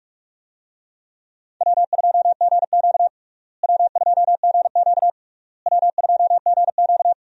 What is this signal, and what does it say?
Morse code, heard as a keyed tone, about 46 wpm, 700 Hz. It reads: W2GQ W2GQ W2GQ
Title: USA Call Signs - 4 Characters - Repeated 3x - 45wpm